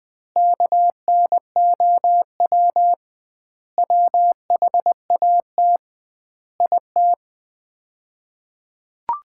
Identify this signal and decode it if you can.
Morse code, heard as a keyed tone, about 20 wpm, 700 Hz. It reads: KNOW WHAT IT
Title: 2x Word-Spacing - Sets of 5 Words - 20wpm